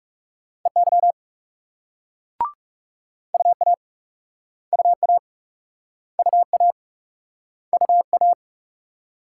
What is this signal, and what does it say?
Morse code, heard as a keyed tone, about 41 wpm, 700 Hz. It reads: EY UA UA UA UA